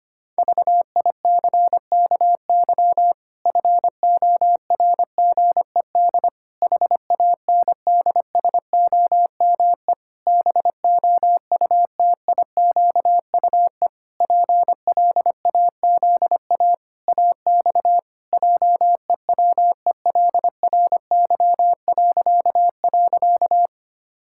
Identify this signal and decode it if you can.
Morse code, heard as a keyed tone, about 25 wpm, 700 Hz. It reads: VICKY FORGED HANDSOME BOUTIQUE PLAZA AX JEWELRY..
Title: Pangram Challenge v2 - 25wpm